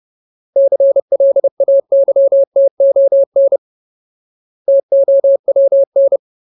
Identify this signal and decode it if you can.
Morse code, heard as a keyed tone, about 30 wpm, 550 Hz. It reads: CLAYTON TOWN
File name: Ultimate Word List / 30WPM Ultimate